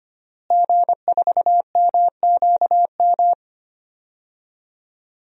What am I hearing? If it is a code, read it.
Morse code, heard as a keyed tone, about 25 wpm, 700 Hz. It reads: G4MQM
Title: DXCC-100 Call Signs - 5 Characters - Repeated 3x - 25wpm